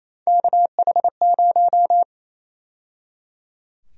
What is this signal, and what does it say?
Morse code, heard as a keyed tone, about 28 wpm, 700 Hz. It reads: KH0
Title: USA Call Sign Prefixes - 28wpm